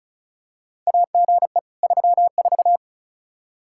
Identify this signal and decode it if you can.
Morse code, heard as a keyed tone, about 34 wpm, 700 Hz. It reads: AGE 34
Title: K1USN SST QSO Practice 35wpm